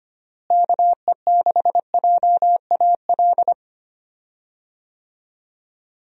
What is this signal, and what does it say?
Morse code, heard as a keyed tone, about 25 wpm, 700 Hz. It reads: KE6JAL
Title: USA Call Signs - 6 Characters - Repeated 2x - 25wpm